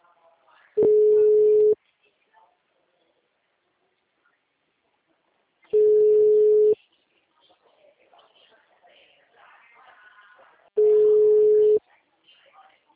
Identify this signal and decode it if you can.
Ringback tone (Europe, 425 Hz), with speech.